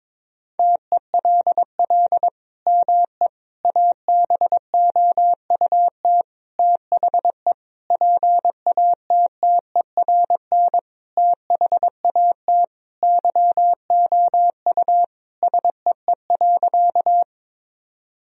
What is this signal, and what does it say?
Morse code, heard as a keyed tone, about 22 wpm, 700 Hz. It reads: TELL ME ABOUT THE PATTERN THAT YOU SEE.
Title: Sentences from Top 1000 Words - Review Part 2 - 22wpm